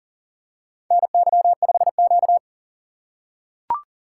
Morse code, heard as a keyed tone, about 40 wpm, 700 Hz. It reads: NY5Q
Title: USA Call Signs - 4 Characters - 2x Character Spacing - 40wpm